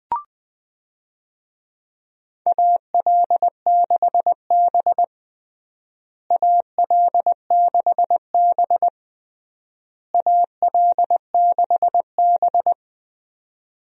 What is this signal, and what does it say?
Morse code, heard as a keyed tone, about 21 wpm, 700 Hz. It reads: AL6B AL6B AL6B
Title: USA Call Signs - 4 Characters - Repeated 3x - 20wpm